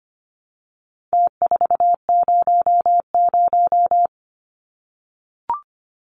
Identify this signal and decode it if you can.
Morse code, heard as a keyed tone, about 25 wpm, 700 Hz. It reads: T400
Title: Numbers - Contest Style - 4-Digit 25wpm